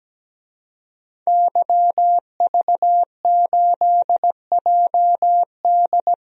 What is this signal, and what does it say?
Morse code, heard as a keyed tone, about 17 wpm, 700 Hz. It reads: YV8JD
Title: DXCC-100 Call Signs - 5 Characters - 17wpm